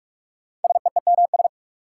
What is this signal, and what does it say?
Morse code, heard as a keyed tone, about 43 wpm, 700 Hz. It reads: SEEMS
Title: Sets of 3 Words - New Word Drill - 45wpm